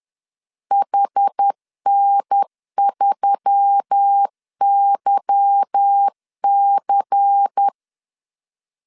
Morse code, heard as a keyed tone, about 10 wpm, 800 Hz. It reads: HN3YC
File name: CW / DET / 1 9 1500C S A 10wpm 4511